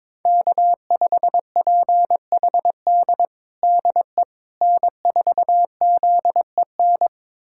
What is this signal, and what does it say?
Morse code, heard as a keyed tone, about 22 wpm, 700 Hz. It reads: K5PHD DE N4ZEN
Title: USA Callsign DE Callsign - 5 Characters - 22wpm